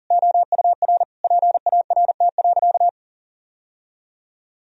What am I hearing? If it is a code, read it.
Morse code, heard as a keyed tone, about 40 wpm, 700 Hz. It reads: OUR PART.
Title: Sentences from Top 1000 Words - Review Part 1 - 40wpm